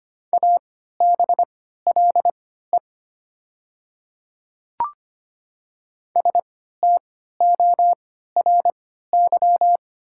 Morse code, heard as a keyed tone, about 25 wpm, spaced as for 14 wpm, 700 Hz. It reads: ABLE STORY
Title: 3x Character Spacing - Top 500 Words - 25wpm